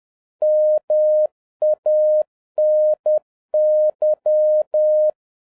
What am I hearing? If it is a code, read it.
Morse code, heard as a keyed tone, about 10 wpm, 610 Hz. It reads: MANY
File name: 2011-07-mp3 / n56 BACK FROM A BIG ACCIDENT 10wpm